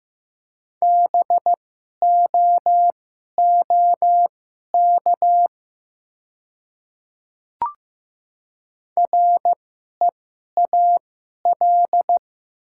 Morse code, heard as a keyed tone, about 15 wpm, spaced as for 11 wpm, 700 Hz. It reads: BOOK REAL
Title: 2x Character Spacing - Top 300 Words - 15wpm